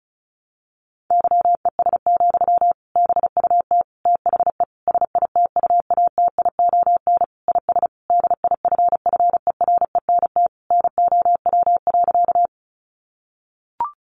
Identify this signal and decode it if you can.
Morse code, heard as a keyed tone, about 34 wpm, 700 Hz. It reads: YES, BUT THE SITUATION IS DIFFERENT NOW.
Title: Sentences from Top 1200 Words 35wpm